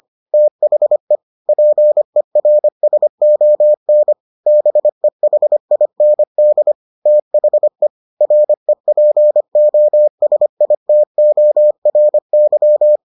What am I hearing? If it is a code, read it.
Morse code, heard as a keyed tone, about 25 wpm, 600 Hz. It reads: THE PERSON BEHIND THE REPOSITORY